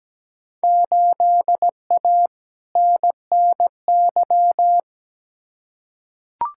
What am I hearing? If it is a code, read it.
Morse code, heard as a keyed tone, about 17 wpm, 700 Hz. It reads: 8A NNY
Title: Field Day Exchanges Encore - 17wpm